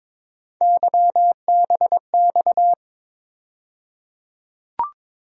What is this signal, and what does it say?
Morse code, heard as a keyed tone, about 21 wpm, 700 Hz. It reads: YBX
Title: USA Call Sign Postfixes - 3 Letters 22wpm